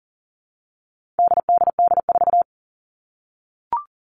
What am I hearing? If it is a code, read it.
Morse code, heard as a keyed tone, about 40 wpm, 700 Hz. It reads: DDD4
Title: All Cut Numbers - 4-Digit 40wpm